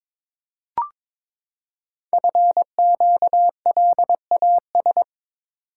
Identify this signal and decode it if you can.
Morse code, heard as a keyed tone, about 22 wpm, 700 Hz. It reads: FQLAS
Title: Code Groups - 5 Letters - 22wpm